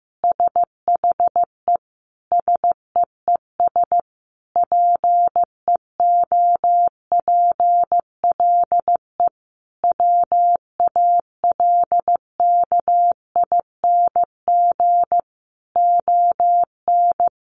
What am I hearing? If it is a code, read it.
Morse code, heard as a keyed tone, about 15 wpm, 700 Hz. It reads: SHE SEES PEOPLE WALKING ON
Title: Easy Phrases 15wpm